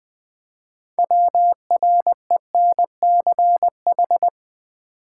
Morse code, heard as a keyed tone, about 20 wpm, 700 Hz. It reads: WRENCH
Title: Easy Words with Hints - 20wpm